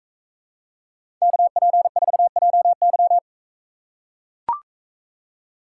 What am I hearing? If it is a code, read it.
Morse code, heard as a keyed tone, about 44 wpm, 700 Hz. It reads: KP4JY